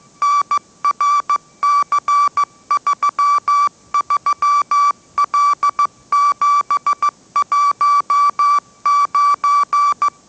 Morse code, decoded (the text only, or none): NRC33L719